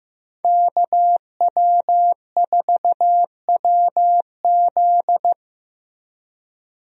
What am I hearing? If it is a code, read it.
Morse code, heard as a keyed tone, about 15 wpm, 700 Hz. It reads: KW4WZ
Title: USA Call Signs - 5 Characters - Repeated 2x - 15wpm